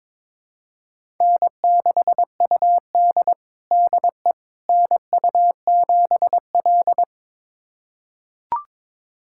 Morse code, heard as a keyed tone, about 23 wpm, 700 Hz. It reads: N6UD DE NU7L
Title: USA Callsign DE Callsign - 4 Characters - 22wpm